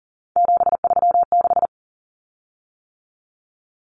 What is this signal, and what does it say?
Morse code, heard as a keyed tone, about 40 wpm, 700 Hz. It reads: 736
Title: Common Cut Numbers - 3-Digit 40wpm